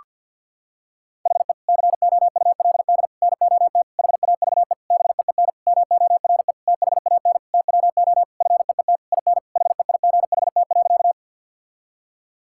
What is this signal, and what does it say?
Morse code, heard as a keyed tone, about 50 wpm, 700 Hz. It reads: HE COULD NOT HAVE BEEN MORE THAN TWO FEET IN HEIGHT.
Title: Sentences from Top 1600 Words 50wpm